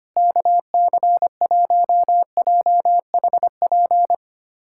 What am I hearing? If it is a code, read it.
Morse code, heard as a keyed tone, about 25 wpm, 700 Hz. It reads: KC1JHP
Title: USA Callsign DE Callsign 25wpm